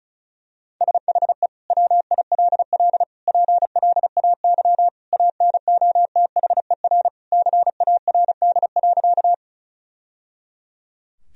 Morse code, heard as a keyed tone, about 34 wpm, 700 Hz. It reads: SHE WILL PLAY ANOTHER CARD.
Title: Sentences from Top 1000 Words - Review Part 4 - 35wpm